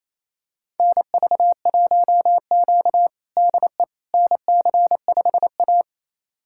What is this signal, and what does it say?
Morse code, heard as a keyed tone, about 28 wpm, 700 Hz. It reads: NV1Q DE NC5A
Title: USA Callsign DE Callsign - 4 Characters - 28wpm